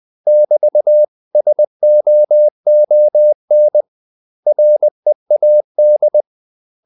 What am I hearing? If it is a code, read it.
Morse code, heard as a keyed tone, about 20 wpm, 600 Hz. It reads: <BT>SOON READ
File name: CW / Practice / 500MostCommonEnglishWords20